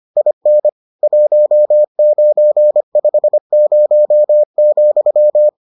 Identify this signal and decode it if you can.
Morse code, heard as a keyed tone, about 25 wpm, 600 Hz. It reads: IN 1950,